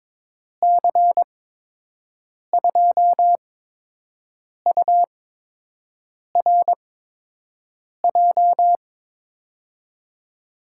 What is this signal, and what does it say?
Morse code, heard as a keyed tone, about 21 wpm, spaced as for 6 wpm, 700 Hz. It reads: C2URJ
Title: USA Call Signs - 6 Characters - 8x Character Spacing - 22wpm